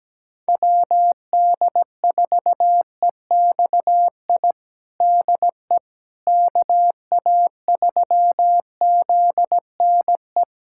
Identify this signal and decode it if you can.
Morse code, heard as a keyed tone, about 17 wpm, 700 Hz. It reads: WD4EXI DE KA3ZNE
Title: USA Callsign DE Callsign - 6 Characters - 17wpm